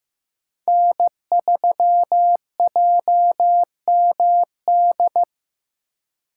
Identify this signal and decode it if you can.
Morse code, heard as a keyed tone, about 15 wpm, 700 Hz. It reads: N3JMD
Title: USA Call Signs - 5 Characters - 3x Character Spacing - 15wpm